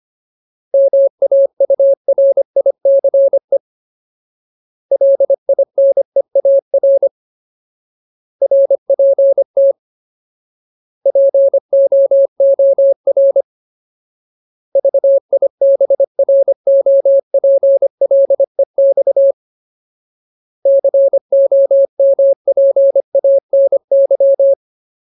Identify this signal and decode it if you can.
Morse code, heard as a keyed tone, about 25 wpm, 550 Hz. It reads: MAURICE LINEAR RPT POOR VIBROPLEX COMPANY
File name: Ultimate Word List / 25WPM Ultimate